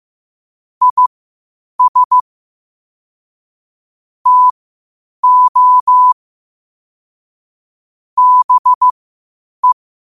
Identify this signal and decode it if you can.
Morse code, heard as a keyed tone, about 15 wpm, spaced as for 8 wpm, 1000 Hz. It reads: IS TO BE
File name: qotdfiles / 8wpm